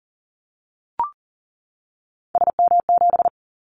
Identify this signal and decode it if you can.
Morse code, heard as a keyed tone, about 39 wpm, 700 Hz. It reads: SM7
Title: Callsign Prefixes for Top 25 DXCC - 40wpm